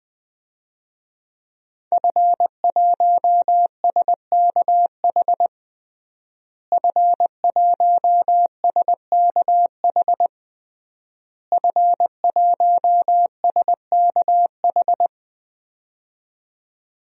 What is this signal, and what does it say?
Morse code, heard as a keyed tone, about 21 wpm, 700 Hz. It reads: F1SKH F1SKH F1SKH
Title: DXCC-100 Call Signs - 5 Characters - Repeated 3x - 20wpm